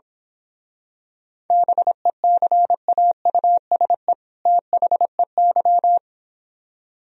Morse code, heard as a keyed tone, about 26 wpm, 700 Hz. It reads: BECAUSE THEY
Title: Speed-Racing - Sets of 2 Words 22wpm